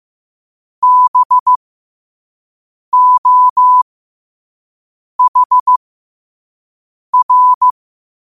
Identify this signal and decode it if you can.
Morse code, heard as a keyed tone, about 15 wpm, spaced as for 5 wpm, 1000 Hz. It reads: BOHR
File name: qotdfiles / 5wpm